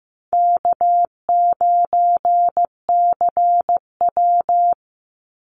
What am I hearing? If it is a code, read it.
Morse code, heard as a keyed tone, about 15 wpm, 700 Hz. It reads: K9CW
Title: Easy Call Signs - 15wpm